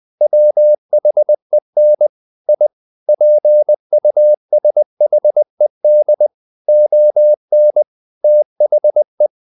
Morse code, heard as a keyed tone, about 20 wpm, 600 Hz. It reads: WHEN I PUSHED ON THE
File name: the old man and the sea part 2 20wpm